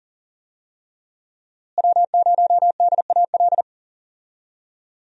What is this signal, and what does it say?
Morse code, heard as a keyed tone, about 38 wpm, 700 Hz. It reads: W0DAL
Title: USA Call Signs - 5 Characters - Repeated 3x - 40wpm